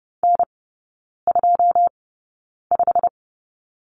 Morse code, heard as a keyed tone, about 30 wpm, spaced as for 9 wpm, 700 Hz. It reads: N25
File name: Lesson 15e - Number 5 - Three Characters 30wpm